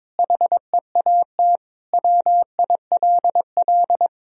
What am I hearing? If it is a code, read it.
Morse code, heard as a keyed tone, about 22 wpm, 700 Hz. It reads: HEAT WILL
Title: Sentences from Top 1000 Words - Review Part 3 - 22wpm